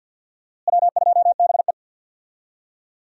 Morse code, heard as a keyed tone, about 48 wpm, 700 Hz. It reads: W2BE
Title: USA Call Signs - 4 Characters - Repeated 2x - 50wpm